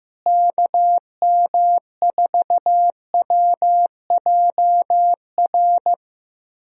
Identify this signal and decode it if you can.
Morse code, heard as a keyed tone, about 15 wpm, 700 Hz. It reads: KM4WJR
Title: USA Call Signs - 6 Characters - Repeated 2x - 15wpm